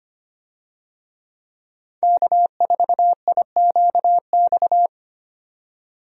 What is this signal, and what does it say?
Morse code, heard as a keyed tone, about 25 wpm, 700 Hz. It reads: K4IQX
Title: USA Call Signs - 5 Characters - 25wpm